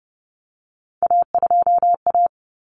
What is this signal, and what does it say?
Morse code, heard as a keyed tone, about 30 wpm, 700 Hz. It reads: A2A
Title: Common Cut Numbers - 3-Digit 30wpm